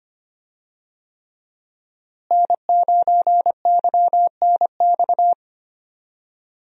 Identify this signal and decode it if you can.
Morse code, heard as a keyed tone, about 25 wpm, 700 Hz. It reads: N9YNX
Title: USA Call Signs - 5 Characters - Repeated 3x - 25wpm